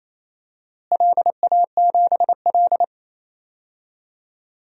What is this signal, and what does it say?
Morse code, heard as a keyed tone, about 27 wpm, 700 Hz. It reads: LA7L
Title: DXCC-100 Call Signs - 4 Characters - Repeated 2x - 28wpm